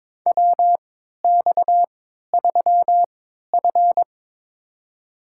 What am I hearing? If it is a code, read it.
Morse code, heard as a keyed tone, about 22 wpm, spaced as for 13 wpm, 700 Hz. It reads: WX3F
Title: USA Call Signs - 4 Characters - 3x Character Spacing - 22wpm